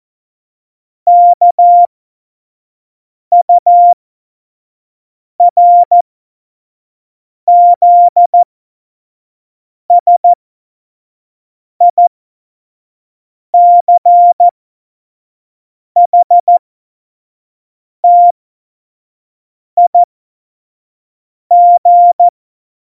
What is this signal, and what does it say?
Morse code, heard as a keyed tone, about 14 wpm, spaced as for 5 wpm, 700 Hz. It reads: KURZSICHTIG
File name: Morsekurs / Witze / Witz 14. November 2017, 25 F70